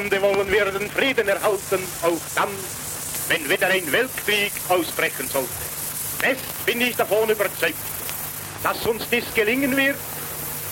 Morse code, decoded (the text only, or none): none